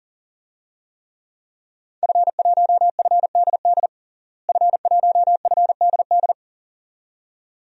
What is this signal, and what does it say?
Morse code, heard as a keyed tone, about 37 wpm, 700 Hz. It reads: F1FDD F1FDD
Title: DXCC-100 Call Signs - 5 Characters - Repeated 2x - 40wpm